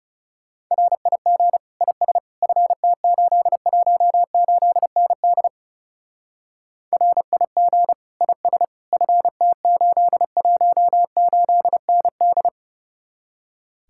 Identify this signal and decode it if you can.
Morse code, heard as a keyed tone, about 37 wpm, 700 Hz. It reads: RIG IS FT818ND RIG IS FT818ND
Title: Speed-Racing Common QSO Phrases Encore - 30wpm